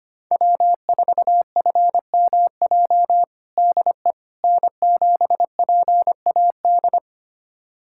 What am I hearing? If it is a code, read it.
Morse code, heard as a keyed tone, about 25 wpm, 700 Hz. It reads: W4FMJ DE N7PAD
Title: USA Callsign DE Callsign - 5 Characters - 25wpm